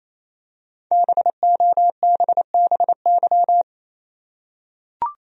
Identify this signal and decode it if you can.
Morse code, heard as a keyed tone, about 28 wpm, 700 Hz. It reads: BOBBY
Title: USA Common Names - 28wpm